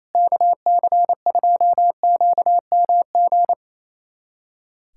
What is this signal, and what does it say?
Morse code, heard as a keyed tone, about 28 wpm, 700 Hz. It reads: KC2QMG
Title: USA Callsign DE Callsign - 6 Characters - 28wpm